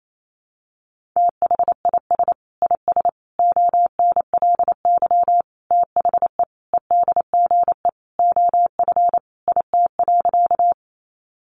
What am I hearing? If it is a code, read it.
Morse code, heard as a keyed tone, about 28 wpm, 700 Hz. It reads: THIS IS ONLY THE EDGE OF IT.
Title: Sentences from Top 1300 Words 28wpm